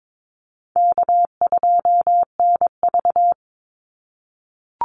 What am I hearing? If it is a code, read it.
Morse code, heard as a keyed tone, about 22 wpm, 700 Hz. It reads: K2NV
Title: Easy Call Signs - 22wpm